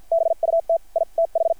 Morse code, heard as a keyed tone, about 50 wpm, spaced as for 44 wpm, 660 Hz. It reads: BUT I TH
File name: archived / CRSnet